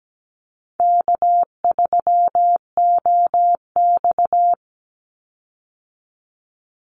Morse code, heard as a keyed tone, about 17 wpm, 700 Hz. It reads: K3OX
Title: Easy Call Signs Repeated 3x - 17wpm